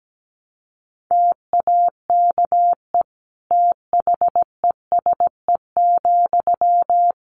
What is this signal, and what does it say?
Morse code, heard as a keyed tone, about 17 wpm, 700 Hz. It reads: TAKE THESE,
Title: Sentences from Top 300 Words 17wpm